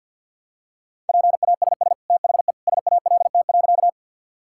Morse code, heard as a keyed tone, about 50 wpm, 700 Hz. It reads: PASS THE SALT.
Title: Sentences from Top 1000 Words - Review Part 5 - 50wpm